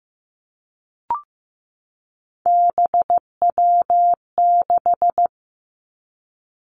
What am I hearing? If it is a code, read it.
Morse code, heard as a keyed tone, about 15 wpm, 700 Hz. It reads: BW6
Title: Three Challenging Letter-Number Combinations 15wpm